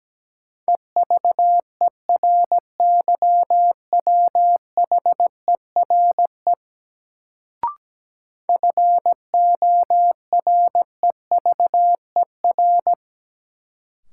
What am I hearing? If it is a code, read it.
Morse code, heard as a keyed tone, about 17 wpm, 700 Hz. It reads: EVERYWHERE FOREVER
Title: Sentences from Top 1600 Words - New Word Drill - 17wpm